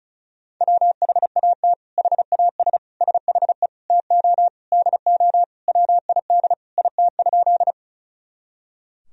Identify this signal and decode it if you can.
Morse code, heard as a keyed tone, about 34 wpm, 700 Hz. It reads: WHAT HAS SHE TO DO WID IT?
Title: Sentences from Top 100 Words and Abbreviations 35wpm